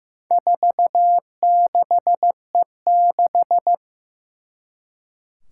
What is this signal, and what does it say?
Morse code, heard as a keyed tone, about 15 wpm, 700 Hz. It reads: 46E6
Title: Four Challenging Letter-Number Combinations 15wpm